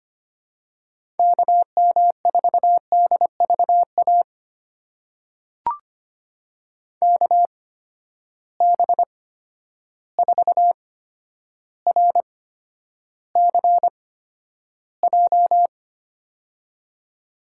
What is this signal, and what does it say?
Morse code, heard as a keyed tone, about 25 wpm, spaced as for 11 wpm, 700 Hz. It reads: KM4DVA KB4RCJ
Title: USA Call Signs - 6 Characters - 8x Character Spacing - 25wpm